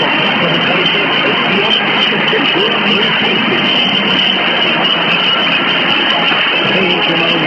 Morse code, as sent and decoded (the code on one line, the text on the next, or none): none